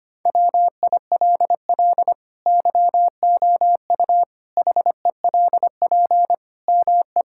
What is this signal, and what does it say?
Morse code, heard as a keyed tone, about 25 wpm, 700 Hz. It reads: WILL YOU HELP ME
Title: Sentences from Top 1000 Words - Review Part 6 - 25wpm